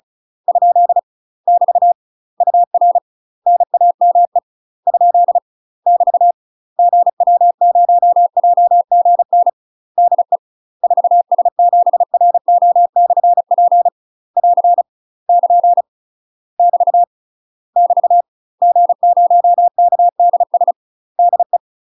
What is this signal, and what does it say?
Morse code, heard as a keyed tone, about 34 wpm, 700 Hz. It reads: ? <BT> UR NAME ? <BT> GW0JGN DE 4S7RO/P <AR> <KN> <BT> <BT> G0KDS DE
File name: UK MASTER QSOs TRAINING/35WPM